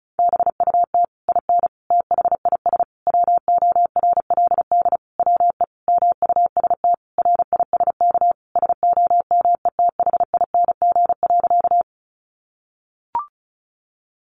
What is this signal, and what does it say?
Morse code, heard as a keyed tone, about 34 wpm, 700 Hz. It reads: BUT IN THIS WORLD WE MUST RISK SOMETHING.
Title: Sentences from Top 600 Words 35wpm